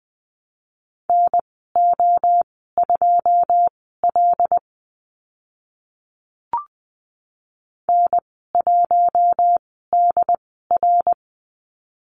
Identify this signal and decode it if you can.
Morse code, heard as a keyed tone, about 20 wpm, spaced as for 15 wpm, 700 Hz. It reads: NO2L N1DR
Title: 082 Lesson 14g - Number 2 - Call Signs 20wpm